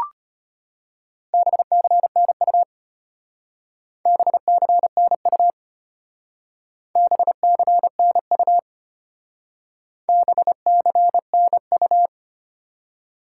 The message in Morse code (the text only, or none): BCNU BCNU BCNU BCNU